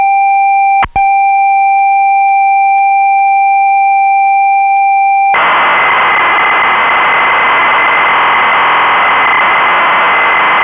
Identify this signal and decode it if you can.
Computer data on cassette.